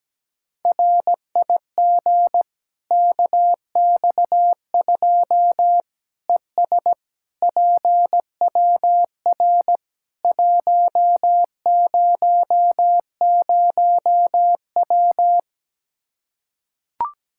Morse code, heard as a keyed tone, about 17 wpm, 700 Hz. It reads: RIG KX2 ES PWR 100W
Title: Practice QSO - Exchange 2 - Info - 17wpm